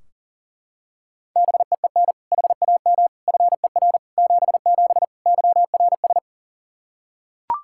Morse code, heard as a keyed tone, about 40 wpm, 700 Hz. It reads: BEEN HAM FER 77 YRS
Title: Practice QSO - Exchange 3 - More Info - 40wpm